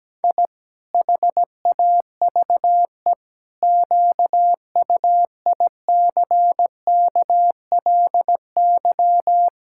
Morse code, heard as a keyed tone, about 17 wpm, 700 Hz. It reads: I HAVE QUICKLY
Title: Pangram Challenge v2 - 17wpm